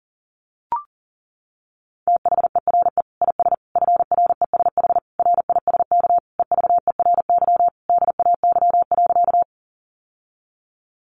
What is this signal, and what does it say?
Morse code, heard as a keyed tone, about 40 wpm, 700 Hz. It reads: THERE IS FRESH RISK EVERY DAY.
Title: Sentences from Top 1200 Words 40wpm